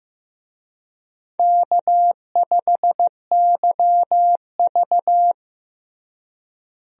Morse code, heard as a keyed tone, about 15 wpm, 700 Hz. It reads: K5YV
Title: USA Call Signs - 4 Characters - Repeated 2x - 15wpm